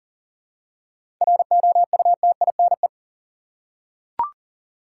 Morse code, heard as a keyed tone, about 40 wpm, 700 Hz. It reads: ROUTINE
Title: Seven Letter Words - 40wpm